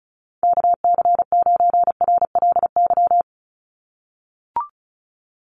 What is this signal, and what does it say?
Morse code, heard as a keyed tone, about 34 wpm, 700 Hz. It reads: KC9RLY
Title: USA Call Signs 35wpm